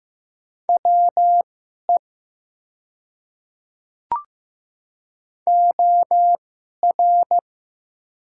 Morse code, heard as a keyed tone, about 15 wpm, spaced as for 11 wpm, 700 Hz. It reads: WE OR
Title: 2x Character Spacing - Top 300 Words - 15wpm